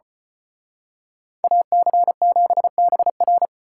Morse code, heard as a keyed tone, about 31 wpm, 700 Hz. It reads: AC7BR
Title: Speed Racing - USA 2x2 Call Signs 25wpm